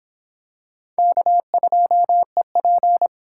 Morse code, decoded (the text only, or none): K2EP